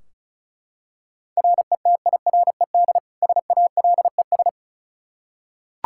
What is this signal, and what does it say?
Morse code, heard as a keyed tone, about 36 wpm, 700 Hz. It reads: RETIRED SALES
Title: Practice QSO - Exchange 3 - More Info - 35wpm